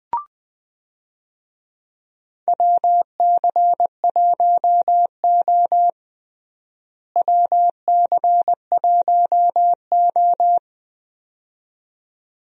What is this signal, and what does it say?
Morse code, heard as a keyed tone, about 20 wpm, 700 Hz. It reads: WC1O WC1O
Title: USA Call Signs - 4 Characters - Repeated 2x - 20wpm